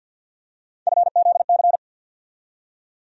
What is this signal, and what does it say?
Morse code, heard as a keyed tone, about 48 wpm, 700 Hz. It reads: FZC
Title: USA Call Sign Postfixes - 3 Letters 50wpm